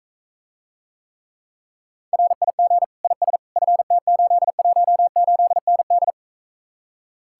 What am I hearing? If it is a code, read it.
Morse code, heard as a keyed tone, about 42 wpm, 700 Hz. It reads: RIG IS FT818ND